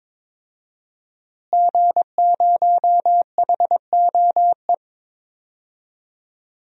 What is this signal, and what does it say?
Morse code, heard as a keyed tone, about 22 wpm, 700 Hz. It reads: G0HOE
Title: DXCC-100 Call Signs - 5 Characters - Repeated 3x - 22wpm